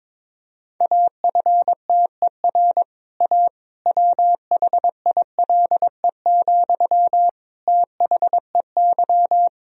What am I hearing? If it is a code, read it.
Morse code, heard as a keyed tone, about 22 wpm, 700 Hz. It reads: AFTER A WHILE, THEY
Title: Sentences from Top 1000 Words - Review Part 1 - 22wpm